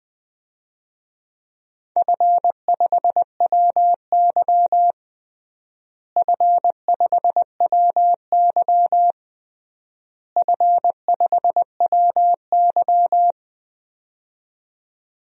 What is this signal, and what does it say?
Morse code, heard as a keyed tone, about 21 wpm, 700 Hz. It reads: F5WY F5WY F5WY
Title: DXCC-100 Call Signs - 4 Characters - Repeated 3x - 20wpm